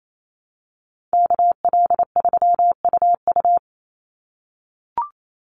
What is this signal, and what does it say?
Morse code, heard as a keyed tone, about 28 wpm, 700 Hz. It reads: KL3UU
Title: USA Call Signs 28wpm